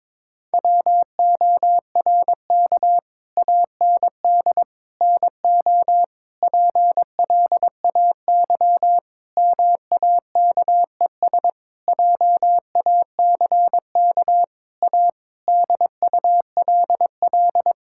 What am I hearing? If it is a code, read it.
Morse code, heard as a keyed tone, about 21 wpm, 700 Hz. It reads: WORK AND NO PLAY MAKES JACK A DULL
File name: practice / 22wpm / Sentences with 8-Word Phrases - 22wpm